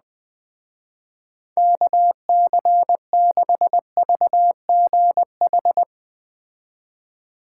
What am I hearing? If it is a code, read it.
Morse code, heard as a keyed tone, about 20 wpm, 700 Hz. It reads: KC6VGH
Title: Speed Racing - USA Call Signs 17wpm